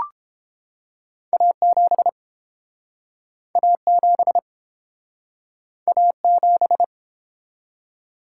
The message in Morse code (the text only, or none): A7 A7 A7